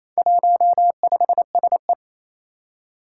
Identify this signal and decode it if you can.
Morse code, heard as a keyed tone, about 27 wpm, 700 Hz. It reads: 15SE
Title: Four Challenging Letter-Number Combinations 28wpm